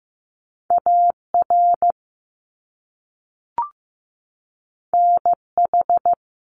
Morse code, heard as a keyed tone, about 15 wpm, 700 Hz. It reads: AR NH
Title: US States 15wpm